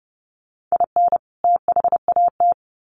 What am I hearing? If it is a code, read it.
Morse code, heard as a keyed tone, about 30 wpm, 700 Hz. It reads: IN THAT
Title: Sets of 2 Words 30wpm